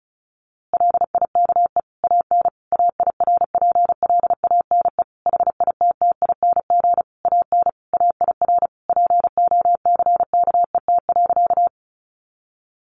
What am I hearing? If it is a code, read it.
Morse code, heard as a keyed tone, about 34 wpm, 700 Hz. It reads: LIKE AN AIRPLANE HITTING AN AIR POCKET.
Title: A Sound Of Thunder 35wpm